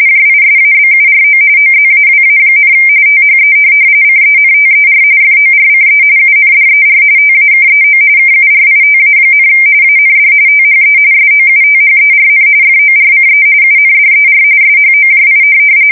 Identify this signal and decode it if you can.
RTTY, about 45.45 baud. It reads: JISABURO OZAWA, THE CENTER FORCE UNDER TAKEO KURITA, AND THE SOUTHERN FORCE UNDER SHOJI
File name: fsk170 / USS-Massachusetts